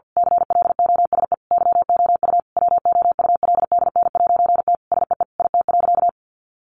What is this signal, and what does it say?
Morse code, heard as a keyed tone, about 50 wpm, 700 Hz. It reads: CLOSE YOU WOULDN'T SEE IT.
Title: A Sound Of Thunder 50wpm